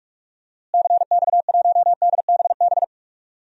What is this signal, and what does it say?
Morse code, heard as a keyed tone, about 42 wpm, 690 Hz. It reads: CX1DBB
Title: DXCC-100 Call Signs - 6 Characters - Repeated 2x - 45wpm